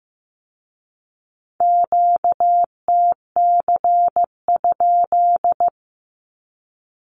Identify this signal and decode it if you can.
Morse code, heard as a keyed tone, about 15 wpm, 700 Hz. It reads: QTC?
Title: Q-Codes with Spoken Meaning - 15wpm